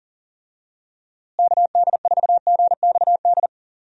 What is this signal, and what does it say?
Morse code, heard as a keyed tone, about 38 wpm, 700 Hz. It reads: KD4GXD
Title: USA Call Signs - 6 Characters - Repeated 2x - 40wpm